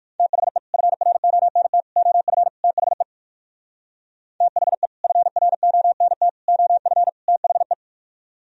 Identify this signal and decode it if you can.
Morse code, heard as a keyed tone, about 58 wpm, 700 Hz. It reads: THE FRONT OF THE THE FRONT OF THE